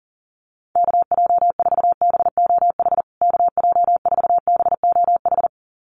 Morse code, heard as a keyed tone, about 40 wpm, 700 Hz. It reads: KJ4BOH KJ4BOH
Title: USA Call Signs Repeated 2x 40wpm